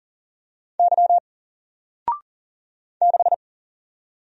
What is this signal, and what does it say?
Morse code, heard as a keyed tone, about 40 wpm, 700 Hz. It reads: Y 6
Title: Single Letter-Number 40wpm